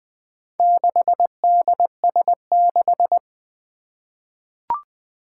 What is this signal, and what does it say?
Morse code, heard as a keyed tone, about 20 wpm, 700 Hz. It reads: 6DS6
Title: Four Challenging Letter-Number Combinations 20wpm